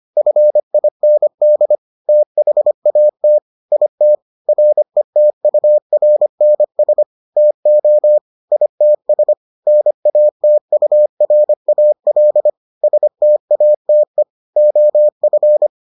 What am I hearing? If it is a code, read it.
Morse code, heard as a keyed tone, about 25 wpm, 600 Hz. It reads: FIND THAT IT RETURNS TO ITS NATURAL STATE OF